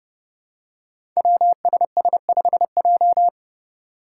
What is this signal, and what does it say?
Morse code, heard as a keyed tone, about 30 wpm, 700 Hz. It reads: WSS5J
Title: Five Challenging Letter-Number Combinations 30wpm